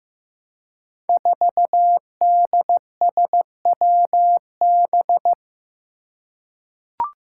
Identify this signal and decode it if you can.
Morse code, heard as a keyed tone, about 15 wpm, 700 Hz. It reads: 4DSWB
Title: Five Challenging Letter-Number Combinations 15wpm